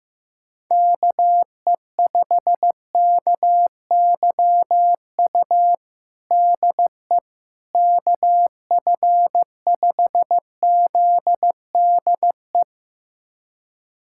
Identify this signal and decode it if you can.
Morse code, heard as a keyed tone, about 15 wpm, 700 Hz. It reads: KE5KYU DE KF5ZDE
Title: USA Callsign DE Callsign 15wpm